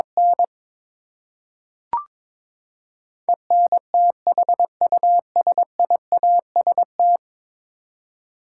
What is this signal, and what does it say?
Morse code, heard as a keyed tone, about 21 wpm, 700 Hz. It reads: N ENTHUSIAST
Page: morse-code-ninja-practice-22wpm